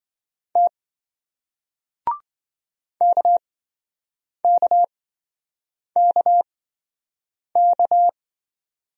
Morse code, heard as a keyed tone, about 27 wpm, spaced as for 7 wpm, 700 Hz. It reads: T KKKK